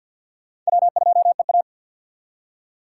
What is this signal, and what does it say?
Morse code, heard as a keyed tone, about 63 wpm, 700 Hz. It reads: W2EA